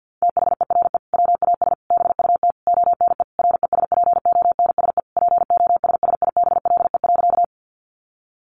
Morse code, heard as a keyed tone, about 50 wpm, 700 Hz. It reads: THERE WAS BUT ONE RESPONSE POSSIBLE.
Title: Sentences from Top 1400 Words 50wpm